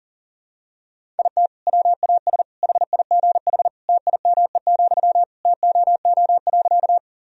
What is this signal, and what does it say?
Morse code, heard as a keyed tone, about 37 wpm, 700 Hz. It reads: IT WAS HIGH TIME, TOO.
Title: Sentences from Top 1000 Words - Review Part 1 - 40wpm